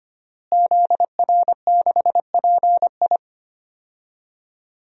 Morse code, heard as a keyed tone, about 25 wpm, 700 Hz. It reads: ZR6PI
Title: DXCC-100 Call Signs - 5 Characters - Repeated 2x - 25wpm